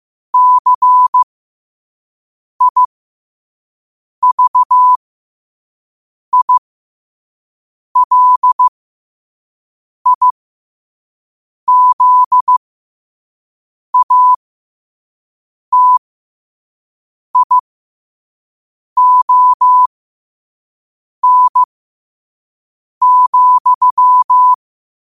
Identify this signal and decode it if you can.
Morse code, heard as a keyed tone, about 15 wpm, spaced as for 5 wpm, 1000 Hz. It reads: CIVILIZATION,
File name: qotdfiles / 5wpm